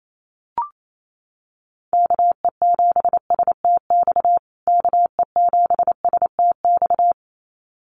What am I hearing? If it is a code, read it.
Morse code, heard as a keyed tone, about 28 wpm, 700 Hz. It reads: KE7STX KE7STX
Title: USA Call Signs Repeated 2x 28wpm